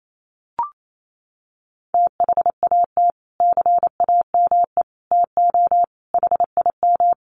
Morse code, heard as a keyed tone, about 28 wpm, 700 Hz. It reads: THAT CAME TO HIM
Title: Sentences from Top 1300 Words 28wpm